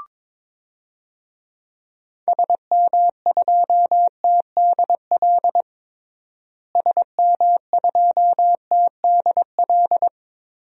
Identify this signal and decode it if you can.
Morse code, heard as a keyed tone, about 22 wpm, 700 Hz. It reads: SM2TDL SM2TDL
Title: DXCC-100 Call Signs - 6 Characters - Repeated 3x - 22wpm